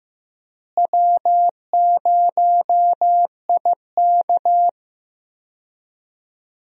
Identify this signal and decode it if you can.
Morse code, heard as a keyed tone, about 15 wpm, 700 Hz. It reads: W0IK
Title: USA Call Signs - 4 Characters - Repeated 3x - 15wpm